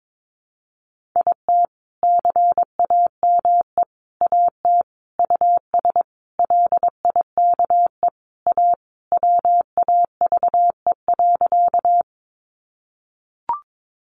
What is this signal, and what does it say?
Morse code, heard as a keyed tone, about 22 wpm, 700 Hz. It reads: IT CAME AT US LIKE A WAVE.
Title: Sentences from Top 1400 Words 22wpm